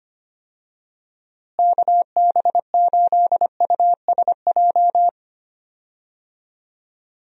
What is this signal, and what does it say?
Morse code, heard as a keyed tone, about 25 wpm, 700 Hz. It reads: KB8USJ
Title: USA Call Signs - 6 Characters - Repeated 3x - 25wpm